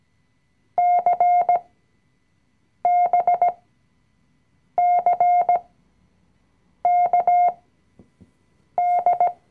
Morse code, heard as a keyed tone, about 17 wpm, spaced as for 6 wpm, 710 Hz. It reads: CBCKD
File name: CW-Online-Kurs / Anfänger / Lektion 020